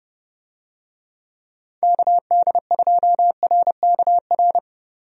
Morse code, heard as a keyed tone, about 31 wpm, 700 Hz. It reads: KD2RKR
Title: USA Call Signs - 6 Characters - Repeated 2x - 30wpm